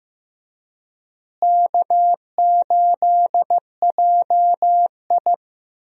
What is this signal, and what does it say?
Morse code, heard as a keyed tone, about 15 wpm, 700 Hz. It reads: K8JI
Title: USA Call Signs - 4 Characters - Repeated 3x - 15wpm